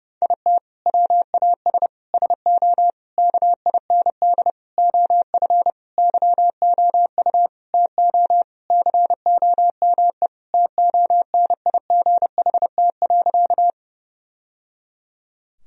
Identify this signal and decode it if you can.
Morse code, heard as a keyed tone, about 30 wpm, 700 Hz. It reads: IT WAS SO KIND OF YOU TO COME TONIGHT.
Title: Sentences from Top 1000 Words - Review Part 3 - 30wpm